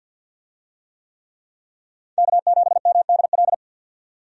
Morse code, heard as a keyed tone, about 47 wpm, 700 Hz. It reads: K7MDL
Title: USA Call Signs - 5 Characters - 50wpm